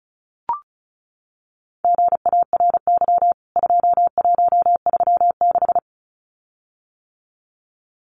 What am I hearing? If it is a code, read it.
Morse code, heard as a keyed tone, about 34 wpm, 700 Hz. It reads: GARY 2136
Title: CWT Exchange 35wpm